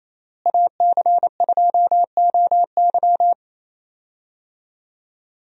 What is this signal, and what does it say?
Morse code, heard as a keyed tone, about 28 wpm, 700 Hz. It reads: AC2OY
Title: USA Call Signs - 5 Characters - Repeated 2x - 28wpm